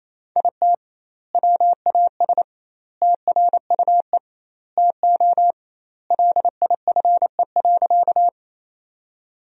Morse code, heard as a keyed tone, about 28 wpm, 700 Hz. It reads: IT WAS TRUE TO LIFE.
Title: 2x Word-Spacing - Sentences from Top 400 Words - 28wpm